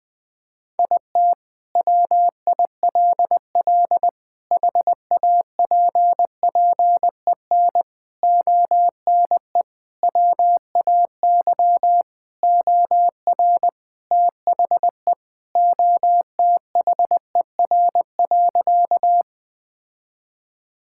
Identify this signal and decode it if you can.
Morse code, heard as a keyed tone, about 20 wpm, 700 Hz. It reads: IT WILL HAPPEN ONE WAY OR THE OTHER.
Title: Sentences with 5-Word Phrases - Spoken Beginning - 20wpm